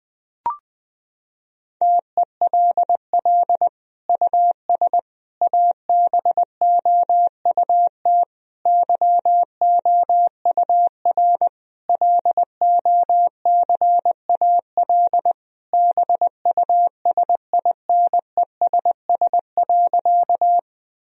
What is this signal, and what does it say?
Morse code, heard as a keyed tone, about 20 wpm, 700 Hz. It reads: TELL US ABOUT YOUR LOCAL BUSINESS.
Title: Sentences from Top 1000 Words - Review Part 3 - 20wpm